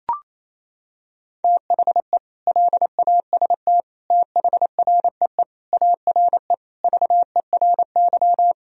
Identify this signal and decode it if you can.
Morse code, heard as a keyed tone, about 28 wpm, 700 Hz. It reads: THE LAST THREE ARE VERY
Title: Sentences from Top 1000 Words - Review Part 1 - 28wpm